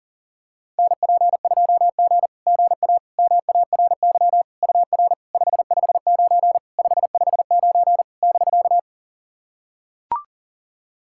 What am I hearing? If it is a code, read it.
Morse code, heard as a keyed tone, about 44 wpm, 700 Hz. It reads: NP2G GA MARY UR 559 559 <BK>
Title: SOTA QSO Practice 40wpm